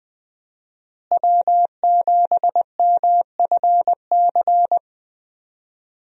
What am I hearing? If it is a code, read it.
Morse code, heard as a keyed tone, about 20 wpm, 700 Hz. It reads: W7MFC
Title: USA Call Signs - 5 Characters - Repeated 3x - 20wpm